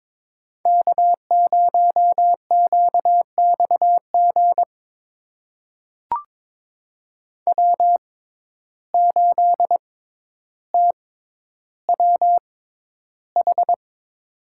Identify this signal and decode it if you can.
Morse code, heard as a keyed tone, about 22 wpm, spaced as for 11 wpm, 700 Hz. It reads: K0QXG W8TWH
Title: USA Call Signs - 5 Characters - 6x Character Spacing - 22wpm